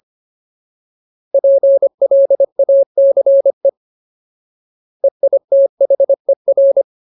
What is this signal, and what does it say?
Morse code, heard as a keyed tone, about 25 wpm, 550 Hz. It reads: PLACE EITHER
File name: Ultimate Word List / 25WPM Ultimate